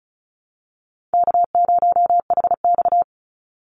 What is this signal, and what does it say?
Morse code, heard as a keyed tone, about 34 wpm, 700 Hz. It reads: K0HX
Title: Call Signs Repeated 2x - 35wpm